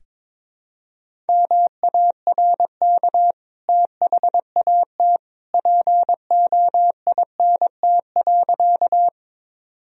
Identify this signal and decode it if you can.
Morse code, heard as a keyed tone, about 21 wpm, 700 Hz. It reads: MARK THAT POINT.
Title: Sentences from Top 1000 Words - Review Part 1 - 22wpm